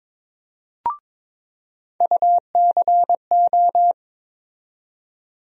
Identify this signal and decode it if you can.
Morse code, heard as a keyed tone, about 21 wpm, 700 Hz. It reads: UCO
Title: USA Call Sign Postfixes - 3 Letters 22wpm